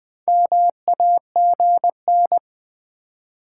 Morse code, heard as a keyed tone, about 20 wpm, 700 Hz. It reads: MAGN
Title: Build A Word - 4 Letters At A Time - 11-Letter Words - 20wpm